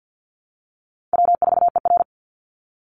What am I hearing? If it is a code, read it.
Morse code, heard as a keyed tone, about 50 wpm, 700 Hz. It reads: W4ER
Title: Easy Call Signs Repeated 2x - 50wpm